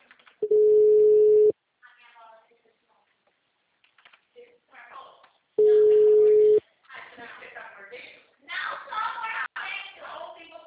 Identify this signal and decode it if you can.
Ringback tone (Europe, 425 Hz), with speech.